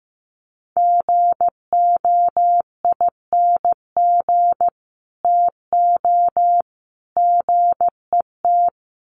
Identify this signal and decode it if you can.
Morse code, heard as a keyed tone, about 15 wpm, 700 Hz. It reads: GOING TO GET
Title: Sets of 3 Words 15wpm